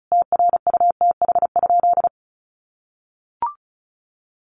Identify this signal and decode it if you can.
Morse code, heard as a keyed tone, about 34 wpm, 700 Hz. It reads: TRUTH?
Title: Sentences from Top 1100 Words 35wpm